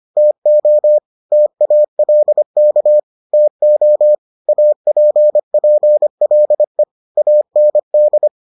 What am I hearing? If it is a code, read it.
Morse code, heard as a keyed tone, about 25 wpm, 600 Hz. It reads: TO TALK TO APPLE AND